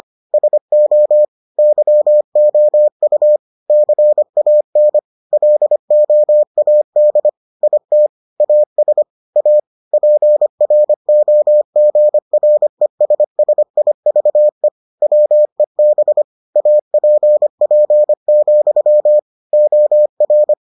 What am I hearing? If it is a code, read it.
Morse code, heard as a keyed tone, about 25 wpm, 600 Hz. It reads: SO YOU CAN LOAD IT AS A PROGRESSIVE WEB APP, OR